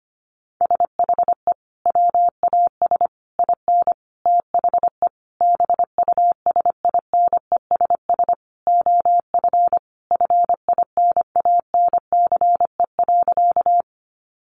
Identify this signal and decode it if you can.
Morse code, heard as a keyed tone, about 25 wpm, 700 Hz. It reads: SHE WAS IN THE BUSINESS OF FINANCE.
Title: Sentences from Top 1500 Words 25wpm